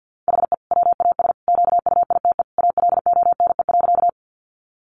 Morse code, heard as a keyed tone, about 50 wpm, 700 Hz. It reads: HE WAS QUITE ALONE.
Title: Sentences from Top 1000 Words 50wpm